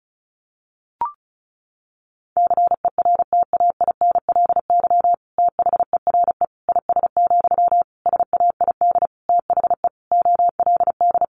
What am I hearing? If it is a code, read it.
Morse code, heard as a keyed tone, about 34 wpm, 700 Hz. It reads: CERTAINLY THERE IS, SAID THE OLD
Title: Sentences from Top 1100 Words 35wpm